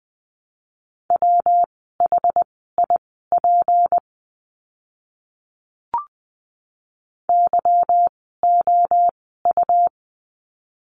Morse code, heard as a keyed tone, about 20 wpm, spaced as for 15 wpm, 700 Hz. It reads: WHIP YOU
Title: 153 Lesson 25f - Letter p - Words 20wpm